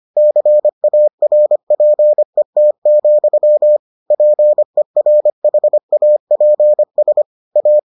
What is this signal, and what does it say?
Morse code, heard as a keyed tone, about 25 wpm, 600 Hz. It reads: CARPET, PERHAPS A